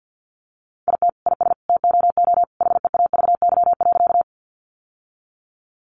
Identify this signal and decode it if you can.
Morse code, heard as a keyed tone, about 50 wpm, 700 Hz. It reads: IT IS TOO HEAVY.
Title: Sentences from Top 1100 Words 50wpm